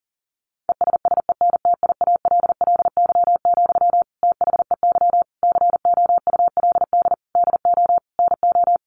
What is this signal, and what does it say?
Morse code, heard as a keyed tone, about 40 wpm, 700 Hz. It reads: ESSENTIALLY, THEY COULD DO NO
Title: Sentences from Top 1300 Words 40wpm